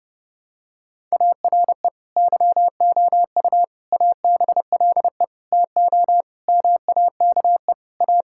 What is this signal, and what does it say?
Morse code, heard as a keyed tone, about 30 wpm, 700 Hz. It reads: ARE YOU ABLE TO MAKE A
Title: Sentences from Top 1000 Words - Review Part 6 - 30wpm